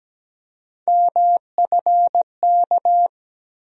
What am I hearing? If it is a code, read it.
Morse code, heard as a keyed tone, about 17 wpm, 700 Hz. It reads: MFK
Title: USA Call Sign Postfixes - 3 Letters 17wpm